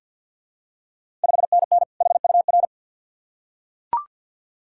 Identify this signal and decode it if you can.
Morse code, heard as a keyed tone, about 47 wpm, 690 Hz. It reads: 5NN HUR
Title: Ontario QSO Party Exchange 50wpm